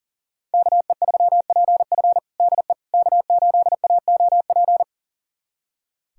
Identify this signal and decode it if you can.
Morse code, heard as a keyed tone, about 37 wpm, 700 Hz. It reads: KE3PF DE K8AOP
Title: USA Callsign DE Callsign - 5 Characters - 40wpm